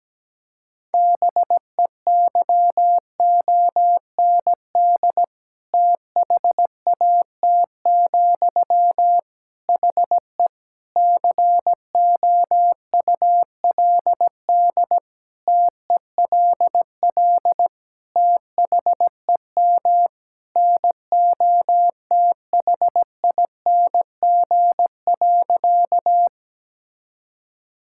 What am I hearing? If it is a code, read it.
Morse code, heard as a keyed tone, about 17 wpm, 700 Hz. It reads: BEYOND THAT, HE COULD TELL THEM NOTHING.
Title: Sentences from Top 1000 Words - Review Part 6 - 17wpm